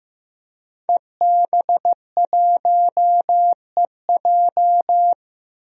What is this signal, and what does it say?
Morse code, heard as a keyed tone, about 15 wpm, 700 Hz. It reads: EB1EJ
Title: DXCC-100 Call Signs - 5 Characters - 15wpm